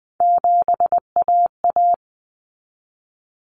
Morse code, heard as a keyed tone, about 20 wpm, 700 Hz. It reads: 7AA
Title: Common Cut Numbers - 3-Digit 20wpm